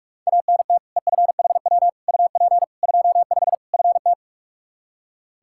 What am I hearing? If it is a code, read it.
Morse code, heard as a keyed tone, about 46 wpm, 700 Hz. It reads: ANT EFHW UP 25 FT
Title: Practice QSO - Exchange 2 - Info - 45wpm